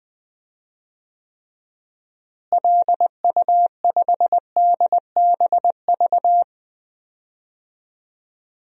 Morse code, heard as a keyed tone, about 20 wpm, 700 Hz. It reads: LU5DBV
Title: DXCC-100 Call Signs - 6 Characters - Repeated 3x - 20wpm